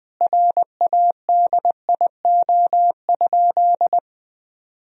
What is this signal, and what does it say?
Morse code, heard as a keyed tone, about 21 wpm, 700 Hz. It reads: RADIO?
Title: Sentences from Top 1000 Words - Review Part 3 - 20wpm